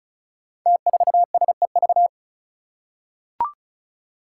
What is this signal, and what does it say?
Morse code, heard as a keyed tone, about 36 wpm, 700 Hz. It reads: T4SEV
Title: Five Challenging Letter-Number Combinations 35wpm